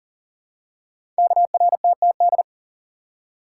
Morse code, heard as a keyed tone, about 43 wpm, 700 Hz. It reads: KRTTD
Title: Code Groups - 5 Letter:Numbers - 40wpm